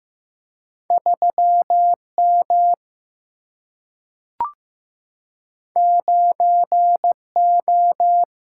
Morse code, heard as a keyed tone, about 15 wpm, 700 Hz. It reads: 3M 9O
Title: Single-Letter Single-Number 15wpm